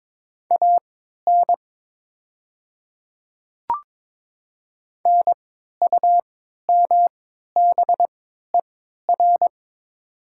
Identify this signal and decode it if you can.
Morse code, heard as a keyed tone, about 22 wpm, spaced as for 12 wpm, 700 Hz. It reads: AN NUMBER
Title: 3x Character Spacing - Top 100 Words - 22wpm